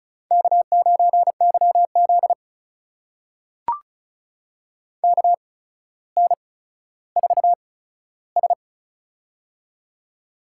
Morse code, heard as a keyed tone, about 36 wpm, spaced as for 15 wpm, 700 Hz. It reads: K9YZ KN4S
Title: USA Call Signs - 4 Characters - 8x Character Spacing - 35wpm